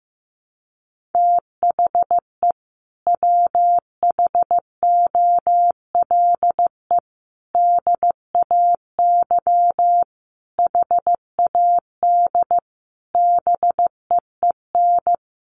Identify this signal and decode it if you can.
Morse code, heard as a keyed tone, about 15 wpm, 700 Hz. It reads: THE WHOLE DAY HAD BEEN
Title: Sentences from Top 500 Words 15wpm